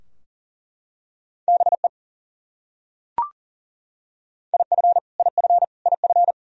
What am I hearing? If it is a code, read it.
Morse code, heard as a keyed tone, about 37 wpm, 700 Hz. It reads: BE IF IF IF
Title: Top 100 Words Repeated 40wpm